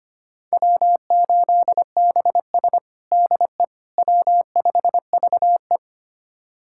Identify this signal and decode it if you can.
Morse code, heard as a keyed tone, about 25 wpm, 700 Hz. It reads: W8BS DE W5VE
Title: USA Callsign DE Callsign - 4 Characters - 25wpm